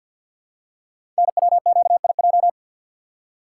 Morse code, heard as a keyed tone, about 48 wpm, 700 Hz. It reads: NWQIJ
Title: Code Groups - 5 Letter:Numbers - 50wpm